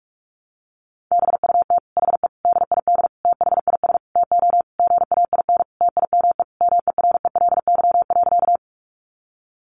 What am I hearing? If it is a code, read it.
Morse code, heard as a keyed tone, about 44 wpm, 700 Hz. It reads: BUT HE DID THIS TO GAIN TIME MERELY.
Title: Sentences from Top 1400 Words 45wpm